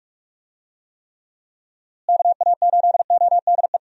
Morse code, heard as a keyed tone, about 46 wpm, 700 Hz. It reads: KA8ODE
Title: USA Call Signs - 6 Characters - Repeated 2x - 45wpm